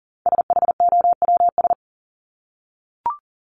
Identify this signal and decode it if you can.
Morse code, heard as a keyed tone, about 38 wpm, 700 Hz. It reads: SHOWS
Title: ES:S Suffix - Top 100 Words with Base Words 40wpm